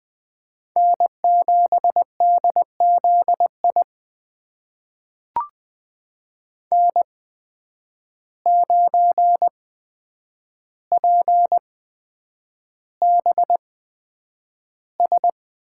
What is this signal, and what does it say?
Morse code, heard as a keyed tone, about 20 wpm, spaced as for 9 wpm, 700 Hz. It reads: N7DZI N9PBS
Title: USA Call Signs - 5 Characters - 8x Character Spacing - 20wpm